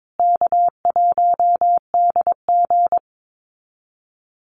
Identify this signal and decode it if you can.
Morse code, heard as a keyed tone, about 22 wpm, 700 Hz. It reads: K1DG
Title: Call Signs Repeated 2x - 22wpm